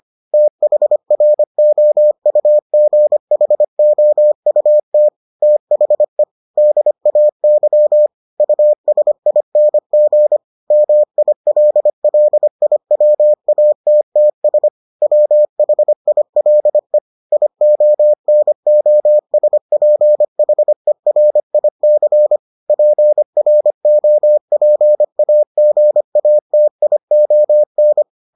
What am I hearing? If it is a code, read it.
Morse code, heard as a keyed tone, about 25 wpm, 600 Hz. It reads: THROUGHOUT THE DAY USING MILLIWATTS WHILE IONOSPHERIC PROPAGATION